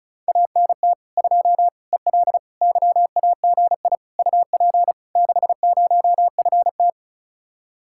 Morse code, heard as a keyed tone, about 43 wpm, 700 Hz. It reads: ANT 2 EL YAGI UP 60FT
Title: Speed-Racing Common QSO Phrases - 35wpm